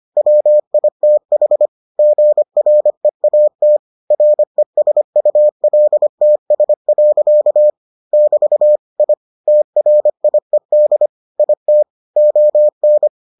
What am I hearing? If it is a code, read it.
Morse code, heard as a keyed tone, about 25 wpm, 600 Hz. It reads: WITH GREAT RESULTS. <BT> I TRIED IT ON